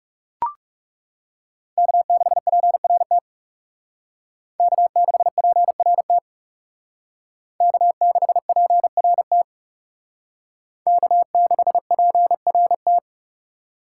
Morse code, heard as a keyed tone, about 36 wpm, 700 Hz. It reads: K6PRT K6PRT K6PRT K6PRT
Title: Speed Racing - USA 1x3 Call Signs 30wpm